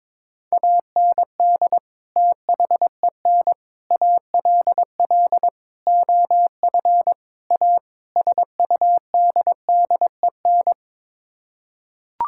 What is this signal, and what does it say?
Morse code, heard as a keyed tone, about 22 wpm, 700 Hz. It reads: AND THEN ALL OF A SUDDEN
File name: practice / 22wpm / Sets of 6 Words - 22wpm